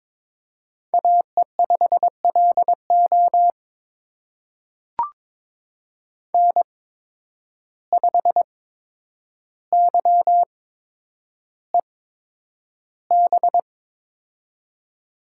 Morse code, heard as a keyed tone, about 22 wpm, spaced as for 9 wpm, 700 Hz. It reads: AE5LO N5YEB
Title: USA Call Signs - 5 Characters - 8x Character Spacing - 22wpm